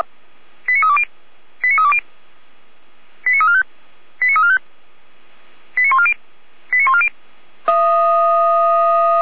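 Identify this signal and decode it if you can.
Five-tone selective call (ZVEI), heard as quick runs of five tones, tones of about 70 ms. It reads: 89320 89320 89345 89345 89140 89140
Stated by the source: ZVEI five-tone call 89305 received with an SDR, associated with feuerwehr-orlamuende (Orlamünde fire brigade).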